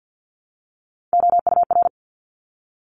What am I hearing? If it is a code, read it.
Morse code, heard as a keyed tone, about 50 wpm, 700 Hz. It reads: OUR
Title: Top 500 Words 50wpm